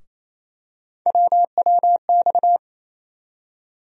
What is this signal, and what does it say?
Morse code, heard as a keyed tone, about 28 wpm, 700 Hz. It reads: WWX
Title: USA Call Sign Postfixes - 3 Letters 28wpm